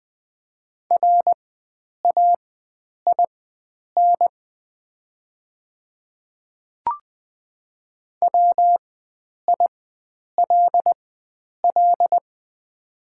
Morse code, heard as a keyed tone, about 20 wpm, spaced as for 9 wpm, 690 Hz. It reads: RAIN WILL
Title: 4x Character Spacing - Top 300 Words - 20wpm